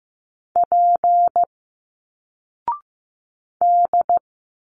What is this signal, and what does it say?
Morse code, heard as a keyed tone, about 15 wpm, 700 Hz. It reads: P D
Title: Single Letters 15wpm